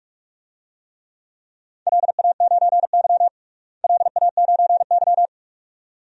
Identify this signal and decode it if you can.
Morse code, heard as a keyed tone, about 46 wpm, 700 Hz. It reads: LA9Y LA9Y
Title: DXCC-100 Call Signs - 4 Characters - Repeated 2x - 45wpm